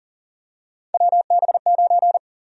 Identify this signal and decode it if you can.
Morse code, heard as a keyed tone, about 37 wpm, 700 Hz. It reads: WB9
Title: USA Call Sign Prefixes - Repeated 2x - 40wpm